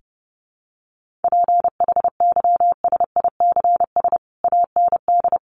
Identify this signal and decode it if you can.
Morse code, heard as a keyed tone, about 30 wpm, 700 Hz. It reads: PHYSICS AND
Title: Sentences from Top 1500 Words 30wpm